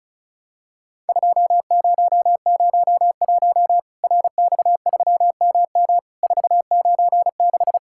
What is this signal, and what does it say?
Morse code, heard as a keyed tone, about 36 wpm, 700 Hz. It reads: 2001 RX3MM 496
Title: Worked All Europe DX Contest - QTC Exchange - 35wpm